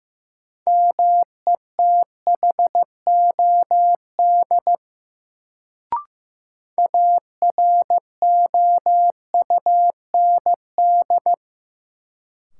Morse code, heard as a keyed tone, about 15 wpm, 700 Hz. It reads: METHOD AROUND
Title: Sentences from Top 600 Words - New Word Drill - 15wpm